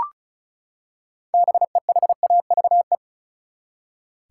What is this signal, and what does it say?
Morse code, heard as a keyed tone, about 36 wpm, 700 Hz. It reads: BEHAVE
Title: Six Letter Words - 35wpm